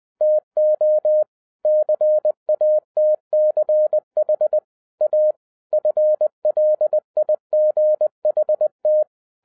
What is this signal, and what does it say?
Morse code, heard as a keyed tone, about 20 wpm, 610 Hz. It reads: TO CATCH A FLIGHT